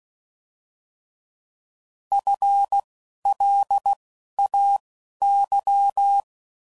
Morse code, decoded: F L A Y

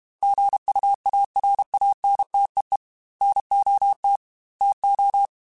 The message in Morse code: GUARANTEE NOT TO

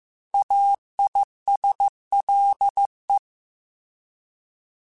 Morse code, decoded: AISLE

